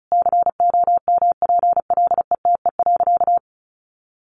Morse code, heard as a keyed tone, about 34 wpm, 700 Hz. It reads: COMPLETE.
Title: Sentences from Top 1300 Words 35wpm